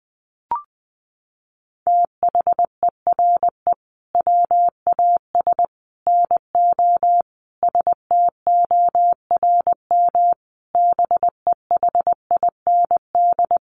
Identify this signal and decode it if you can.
Morse code, heard as a keyed tone, about 20 wpm, 700 Hz. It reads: THERE WAS NO STORM BEHIND
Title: Sentences from Top 1500 Words 20wpm